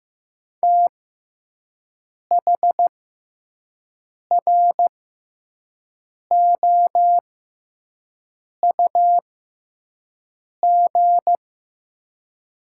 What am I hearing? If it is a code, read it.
Morse code, heard as a keyed tone, about 15 wpm, spaced as for 5 wpm, 700 Hz. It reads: THROUG
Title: 6x Character Spacing - Top 200 Words - 15wpm